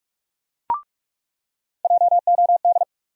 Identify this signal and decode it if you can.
Morse code, heard as a keyed tone, about 44 wpm, 700 Hz. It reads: JOD
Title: USA Call Sign Postfixes - 3 Letters 45wpm